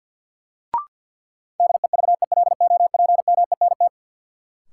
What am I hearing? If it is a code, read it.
Morse code, heard as a keyed tone, about 46 wpm, 700 Hz. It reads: DEVELOPMENT
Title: Sentences from Top 1400 Words - New Word Drill Alt - 50wpm